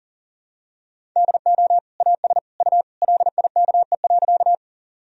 Morse code, heard as a keyed tone, about 37 wpm, 700 Hz. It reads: DO AS U LIKE.
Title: Sentences from Top 100 Words and Abbreviations 40wpm